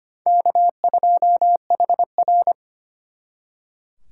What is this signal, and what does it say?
Morse code, heard as a keyed tone, about 32 wpm, 700 Hz. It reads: K2HR